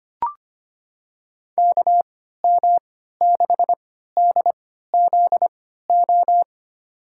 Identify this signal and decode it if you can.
Morse code, heard as a keyed tone, about 25 wpm, spaced as for 14 wpm, 700 Hz. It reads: KM6DZO
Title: USA Call Signs - 6 Characters - 3x Character Spacing - 25wpm